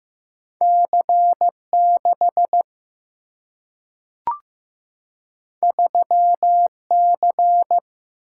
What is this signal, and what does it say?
Morse code, heard as a keyed tone, about 15 wpm, 700 Hz. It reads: C6 3C
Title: Single-Letter Single-Number 15wpm